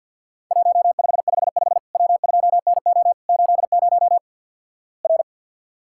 Morse code, heard as a keyed tone, about 48 wpm, 700 Hz. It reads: 1555 W2NO 80 R
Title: Worked All Europe DX Contest - QTC Exchange - 50wpm